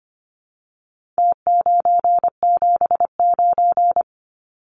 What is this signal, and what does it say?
Morse code, heard as a keyed tone, about 25 wpm, 700 Hz. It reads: T979
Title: Numbers - Contest Style - 4-Digit 25wpm